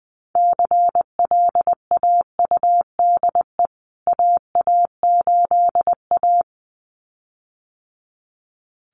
Morse code, heard as a keyed tone, about 19 wpm, 700 Hz. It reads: CLAUDE AA8A
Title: CWT Exchange 20wpm